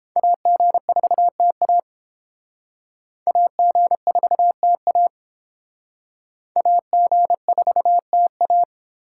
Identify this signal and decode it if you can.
Morse code, heard as a keyed tone, about 28 wpm, 700 Hz. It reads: AG4TA AG4TA AG4TA